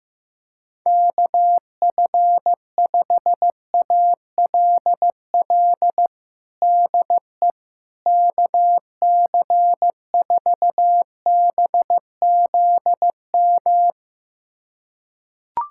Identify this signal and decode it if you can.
Morse code, heard as a keyed tone, about 15 wpm, 700 Hz. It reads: KF5ALL DE KC4BZM
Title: USA Callsign DE Callsign - 6 Characters - 15wpm